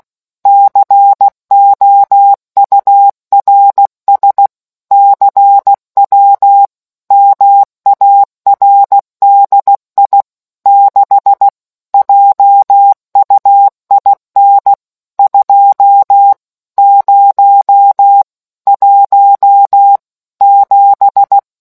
Morse code, heard as a keyed tone, about 16 wpm, 800 Hz. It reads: COURS CW MARDI 6 JUIN 2 0 1 7